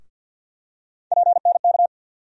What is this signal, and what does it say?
Morse code, heard as a keyed tone, about 52 wpm, 690 Hz. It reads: PNK
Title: USA Call Sign Postfixes - 3 Letters 50wpm